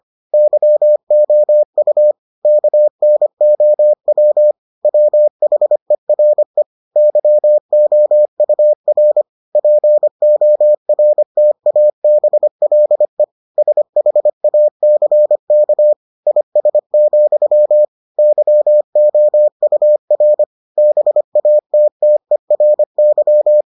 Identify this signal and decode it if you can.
Morse code, heard as a keyed tone, about 25 wpm, 600 Hz. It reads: YOU KNOW WHERE YOUR PORTABLE SHACK IS, YOUR BATTERY